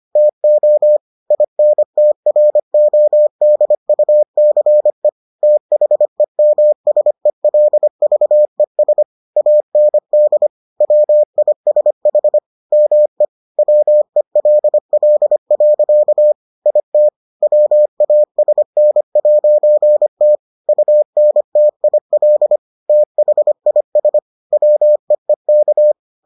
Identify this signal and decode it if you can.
Morse code, heard as a keyed tone, about 25 wpm, 600 Hz. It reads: TO INTRODUCE THEMSELVES AND WISH ME WELL. IT WASN'T UNTIL THIS WEEK